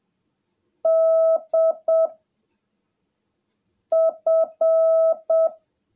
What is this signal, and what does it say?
Morse code, heard as a keyed tone, about 7 wpm, 660 Hz. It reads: D F
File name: jh8cbh / 2021cw / cw17a